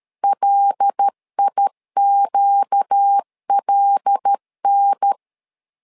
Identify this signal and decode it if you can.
Morse code, heard as a keyed tone, about 13 wpm, 800 Hz. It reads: LIQLN
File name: LEZIONI / 5 LBVC / 5 L 12wpm 1913